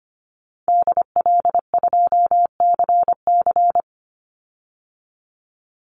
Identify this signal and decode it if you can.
Morse code, heard as a keyed tone, about 25 wpm, 700 Hz. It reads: DL2CC
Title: Call Signs - 25wpm